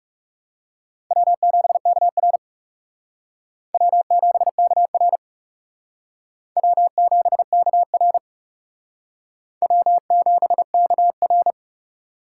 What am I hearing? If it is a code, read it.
Morse code, heard as a keyed tone, about 36 wpm, 700 Hz. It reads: W7KR W7KR W7KR W7KR
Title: Speed Racing - USA 1x2 Call Signs 30wpm